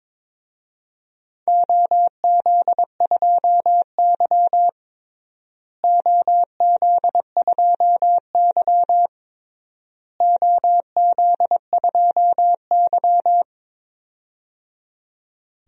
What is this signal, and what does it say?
Morse code, heard as a keyed tone, about 22 wpm, 700 Hz. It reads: OZ2Y OZ2Y OZ2Y
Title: DXCC-100 Call Signs - 4 Characters - Repeated 3x - 22wpm